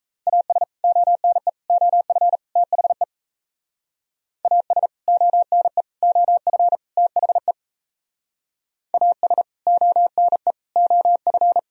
Morse code, heard as a keyed tone, about 33 wpm, 700 Hz. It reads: AS ONE OF THE AS ONE OF THE AS ONE OF
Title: Speed-Racing - Sets of 4 Words 28wpm